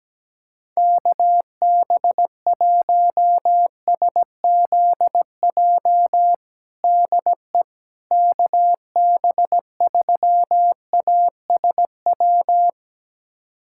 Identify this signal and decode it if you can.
Morse code, heard as a keyed tone, about 17 wpm, 700 Hz. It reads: KB1SZJ DE KB3ASW
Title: USA Callsign DE Callsign - 6 Characters - 17wpm